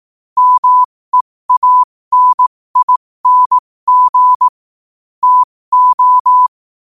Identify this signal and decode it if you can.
Morse code, heard as a keyed tone, about 18 wpm, spaced as for 15 wpm, 1000 Hz. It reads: MEANING TO
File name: qotdfiles / 15wpm